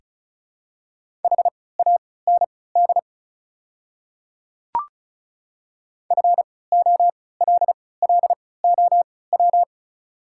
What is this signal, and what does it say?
Morse code, heard as a keyed tone, about 34 wpm, spaced as for 20 wpm, 700 Hz. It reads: HAND FOLLOW
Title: 3x Character Spacing - Top 200 Words - 35wpm